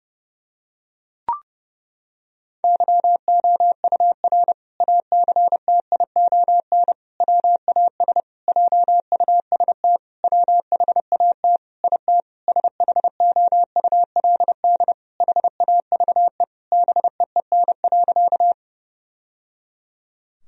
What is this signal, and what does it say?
Morse code, heard as a keyed tone, about 30 wpm, 700 Hz. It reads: YOUR ACTION WAS JUST WHAT IT SHOULD HAVE BEEN.
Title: Sentences from Top 1000 Words - Review Part 4 - 30wpm